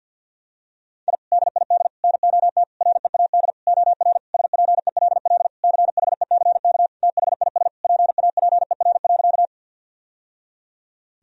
Morse code, heard as a keyed tone, about 52 wpm, 700 Hz. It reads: I DID NOT READ OR SPELL CHECK THIS PAPER.
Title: Sentences from Top 1000 Words - Review Part 3 - 50wpm